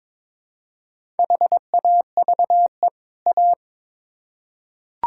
Morse code, heard as a keyed tone, about 22 wpm, 700 Hz. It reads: HAVE A
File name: Build A Phrase - 2-Word Phrases - 22wpm